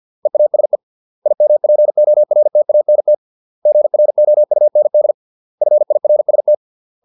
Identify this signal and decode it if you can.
Morse code, heard as a keyed tone, about 50 wpm, 600 Hz. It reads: EASE IMPORTANT GROUND FIRST
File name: 632 words / 632 Words 50WPM